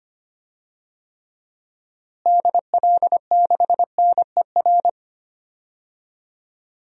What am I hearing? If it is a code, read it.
Morse code, heard as a keyed tone, about 25 wpm, 700 Hz. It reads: DL6NER
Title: DXCC-100 Call Signs - 6 Characters - Repeated 3x - 25wpm